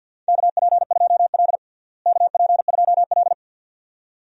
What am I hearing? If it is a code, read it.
Morse code, heard as a keyed tone, about 50 wpm, 700 Hz. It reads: KP2L KP2L
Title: DXCC-100 Call Signs - 4 Characters - Repeated 2x - 50wpm